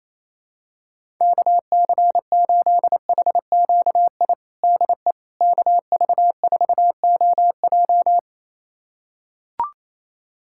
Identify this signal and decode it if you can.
Morse code, heard as a keyed tone, about 28 wpm, 700 Hz. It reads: KC8HQI DE KV4OJ
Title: USA Callsign DE Callsign 28wpm